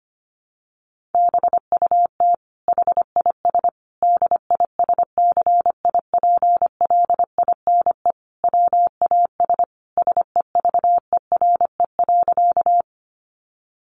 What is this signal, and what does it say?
Morse code, heard as a keyed tone, about 25 wpm, 700 Hz. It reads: BUT HIS DISCIPLINE WAS SEVERE.
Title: Sentences from Top 1400 Words 25wpm